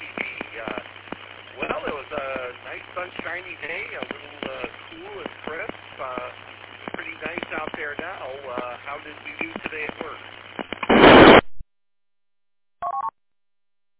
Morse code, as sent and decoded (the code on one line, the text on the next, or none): none